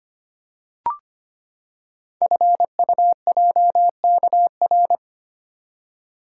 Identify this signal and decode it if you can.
Morse code, heard as a keyed tone, about 25 wpm, 700 Hz. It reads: FUJKR
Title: Code Groups - 5 Letters - 25wpm